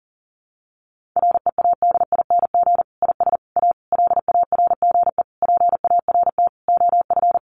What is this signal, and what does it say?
Morse code, heard as a keyed tone, about 40 wpm, 700 Hz. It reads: READING IS A LARGE PART OF
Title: Sentences from Top 1400 Words 40wpm